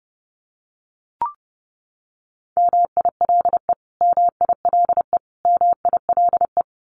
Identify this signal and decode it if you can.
Morse code, heard as a keyed tone, about 30 wpm, 700 Hz. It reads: MILE MILE MILE
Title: Top 300 Words Repeated 30wpm